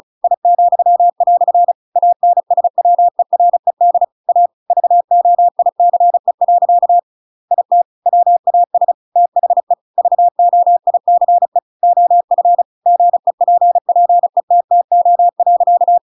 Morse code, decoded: I," ANSWERED A VOICE. IT WAS THE VOICE OF GEPPETTO.

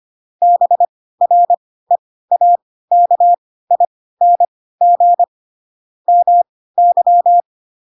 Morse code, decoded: BREAKING MY

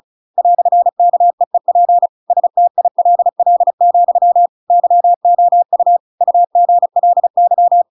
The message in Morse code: "KEEP STILL, YOU UGLY

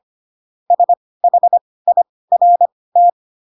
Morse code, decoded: SHIRT

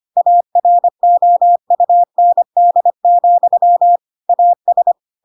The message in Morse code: AROUND, AS

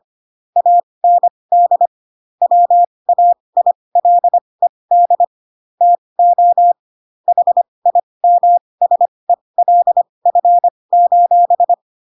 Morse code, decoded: AND WAILED TO HIMSELF:

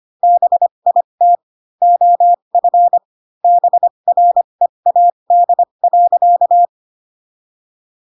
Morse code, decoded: BIT OF BREAD.

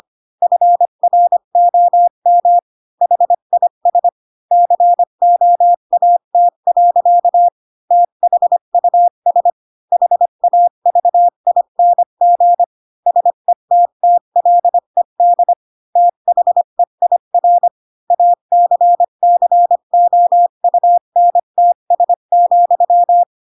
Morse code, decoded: FROM HIS COAT. THUS HAVING SETTLED THEIR ACCOUNTS,

